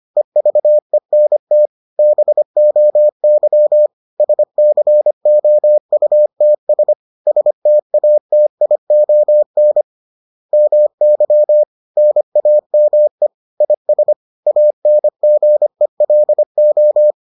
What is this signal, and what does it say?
Morse code, heard as a keyed tone, about 25 wpm, 600 Hz. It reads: EVENT BOY SCOUTS STATION MY NAME IS ANGELO